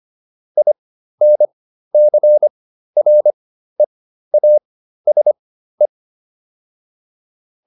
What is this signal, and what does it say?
Morse code, heard as a keyed tone, about 25 wpm, spaced as for 13 wpm, 600 Hz. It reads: INCREASE